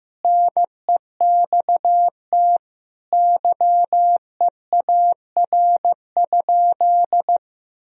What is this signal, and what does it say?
Morse code, heard as a keyed tone, about 15 wpm, 700 Hz. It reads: NEXT YEAR?
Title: Sentences from Top 1000 Words - Review Part 3 - 15wpm